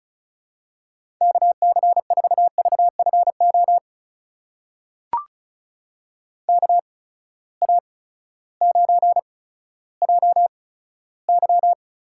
Morse code, heard as a keyed tone, about 34 wpm, spaced as for 16 wpm, 700 Hz. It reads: KC4VFO KA9JY
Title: USA Call Signs - 6 Characters - 8x Character Spacing - 35wpm